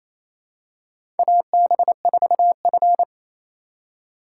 Morse code, heard as a keyed tone, about 28 wpm, 700 Hz. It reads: AB4F
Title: USA Call Signs - 4 Characters - 3x Character Spacing - 28wpm